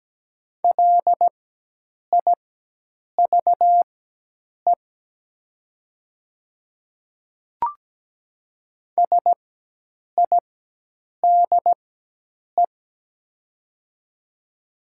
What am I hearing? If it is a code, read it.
Morse code, heard as a keyed tone, about 17 wpm, spaced as for 8 wpm, 700 Hz. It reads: LIVE SIDE
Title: 4x Character Spacing - Top 300 Words - 17wpm